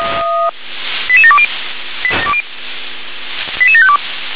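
Five-tone selective call (ZVEI), heard as quick runs of five tones, tones of about 70 ms. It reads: 88720 88752
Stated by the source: ZVEI five-tone call 88756 received by SDR.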